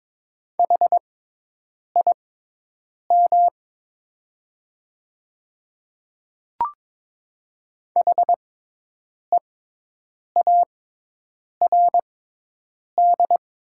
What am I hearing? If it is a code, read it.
Morse code, heard as a keyed tone, about 22 wpm, spaced as for 8 wpm, 700 Hz. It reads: HIM HEARD
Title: 6x Character Spacing - Top 500 Words - 22wpm